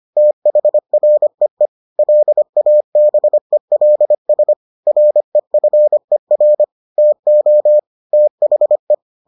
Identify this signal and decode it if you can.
Morse code, heard as a keyed tone, about 25 wpm, 600 Hz. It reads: THREE LABELS REFER TO THE